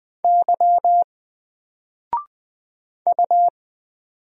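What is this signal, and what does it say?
Morse code, heard as a keyed tone, about 20 wpm, 700 Hz. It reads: Y U
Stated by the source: Single letter and number Morse drill at 20 wpm, from N4LCD.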